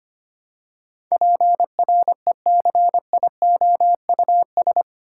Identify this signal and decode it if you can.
Morse code, heard as a keyed tone, about 25 wpm, 690 Hz. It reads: PRECIOUS
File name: 8 Letter Words - First 3 Letters Given - 25wpm